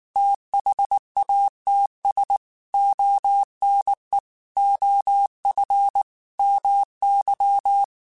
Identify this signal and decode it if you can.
Morse code, heard as a keyed tone, about 19 wpm, 800 Hz. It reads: THATS ONE OF MY
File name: QSOs / QSOs-mp3s / WD0FBT 1@18